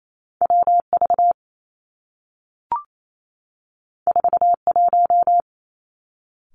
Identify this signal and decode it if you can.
Morse code, heard as a keyed tone, about 28 wpm, 700 Hz. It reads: W V 4 1
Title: Two Challenging Letter-Number Combinations 28wpm